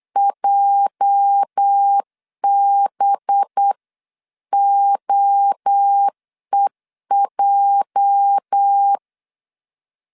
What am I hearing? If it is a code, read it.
Morse code, heard as a keyed tone, about 8 wpm, 800 Hz. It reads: JBOEJ
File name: LEZIONI / 8 YJ45 / 8 YJ 8wpm 3313